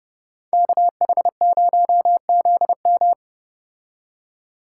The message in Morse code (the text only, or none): KH0ZM